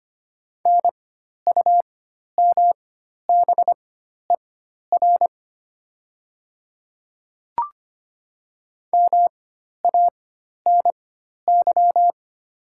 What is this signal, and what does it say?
Morse code, heard as a keyed tone, about 25 wpm, spaced as for 12 wpm, 700 Hz. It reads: NUMBER MANY
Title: 4x Character Spacing - Top 100 Words - 25wpm